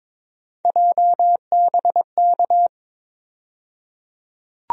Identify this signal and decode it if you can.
Morse code, heard as a keyed tone, about 21 wpm, 700 Hz. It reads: JBK
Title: USA Call Sign Postfixes - 3 Letters 22wpm